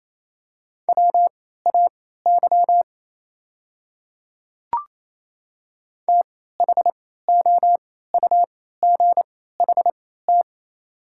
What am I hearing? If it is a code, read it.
Morse code, heard as a keyed tone, about 28 wpm, spaced as for 16 wpm, 700 Hz. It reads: WAY THOUGHT
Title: 3x Character Spacing - Top 300 Words - 28wpm